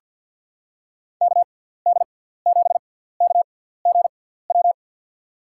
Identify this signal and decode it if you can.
Morse code, heard as a keyed tone, about 47 wpm, spaced as for 17 wpm, 700 Hz. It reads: KD7KGW
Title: USA Call Signs - 6 Characters - 6x Character Spacing - 50wpm